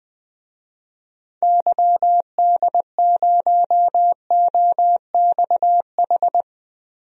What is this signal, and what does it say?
Morse code, heard as a keyed tone, about 20 wpm, 700 Hz. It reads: YD0OXH
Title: DXCC-100 Call Signs - 6 Characters - Repeated 2x - 20wpm